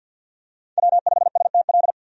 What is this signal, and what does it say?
Morse code, heard as a keyed tone, about 52 wpm, 700 Hz. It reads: W5STL
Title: USA Call Signs - 5 Characters - 50wpm